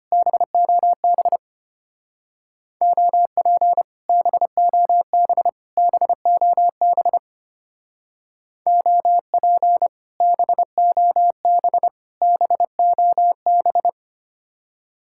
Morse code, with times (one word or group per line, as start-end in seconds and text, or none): 0.1-1.4 BOB
2.8-3.8 OP
4.1-5.5 BOB
5.8-7.2 BOB
8.7-9.9 OP
10.2-11.9 BOB
12.2-13.9 BOB